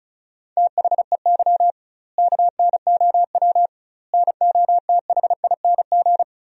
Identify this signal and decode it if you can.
Morse code, heard as a keyed tone, about 36 wpm, 700 Hz. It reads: THEY KNOW NOTHING
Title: 2x Word-Spacing - Sentences from Top 500 Words - 35wpm